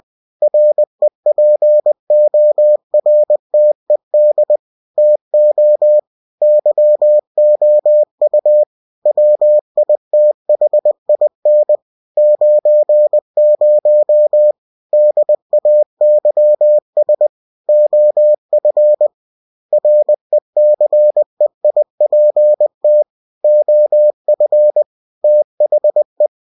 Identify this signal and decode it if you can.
Morse code, heard as a keyed tone, about 20 wpm, 600 Hz. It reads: REPORTED TO YOU WITHIN 90 DAYS OF RECEIPT OF THE
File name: cw-books / winnie-the-pooh / winnie-20wpm-page-0117